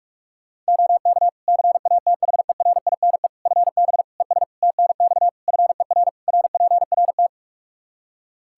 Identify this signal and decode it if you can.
Morse code, heard as a keyed tone, about 41 wpm, 700 Hz. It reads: OK CATHERINE FB ES TNX FER RPRT
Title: Practice QSO - Exchange 2 - Info - 45wpm